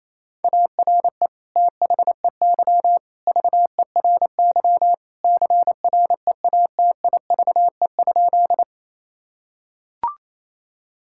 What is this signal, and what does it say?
Morse code, heard as a keyed tone, about 28 wpm, 700 Hz. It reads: ARE THEY VERY CREATIVE?
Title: Sentences from Top 1000 Words - Review Part 5 - 28wpm